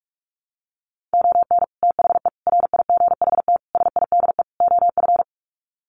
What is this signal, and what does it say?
Morse code, heard as a keyed tone, about 44 wpm, 700 Hz. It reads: ON THE RIGHT SIDE OF
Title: Sets of 5 Words Encore 45wpm